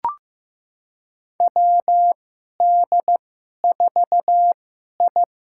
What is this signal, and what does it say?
Morse code, heard as a keyed tone, about 15 wpm, spaced as for 11 wpm, 700 Hz. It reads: WD4I
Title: USA Call Signs - 2x Character Spacing - 15wpm